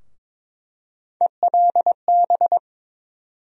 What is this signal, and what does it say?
Morse code, heard as a keyed tone, about 22 wpm, 700 Hz. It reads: ELB
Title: USA Call Sign Postfixes - 3 Letters 22wpm